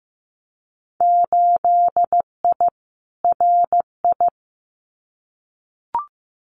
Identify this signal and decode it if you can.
Morse code, heard as a keyed tone, about 15 wpm, 700 Hz. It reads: 8I RI
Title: Winter Field Day Exchanges Encore 15wpm